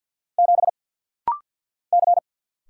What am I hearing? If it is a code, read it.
Morse code, heard as a keyed tone, about 50 wpm, 700 Hz. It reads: 7 C
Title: Single Letter-Number Alt - Rapid-Fire 50wpm